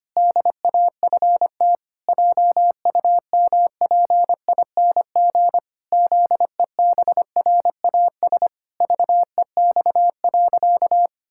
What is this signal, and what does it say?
Morse code, heard as a keyed tone, about 25 wpm, 700 Hz. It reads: DAFT JUMPING ZEBRAS VEX.
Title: Pangram Challenge v2 - 25wpm